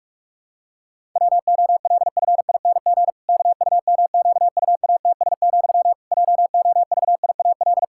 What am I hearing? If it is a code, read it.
Morse code, heard as a keyed tone, about 46 wpm, 700 Hz. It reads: WOLFING KUMQUATS, JOVIAL
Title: Pangram Challenge v2 - 45wpm